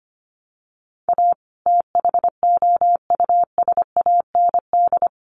Morse code, heard as a keyed tone, about 25 wpm, 700 Hz. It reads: A THOUSAND
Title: Sentences from Top 600 Words 25wpm